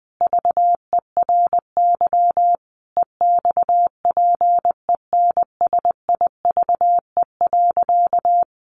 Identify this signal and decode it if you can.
Morse code, heard as a keyed tone, about 20 wpm, 700 Hz. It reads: VERY EXPENSIVE.
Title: Additional Easy Phrases 20wpm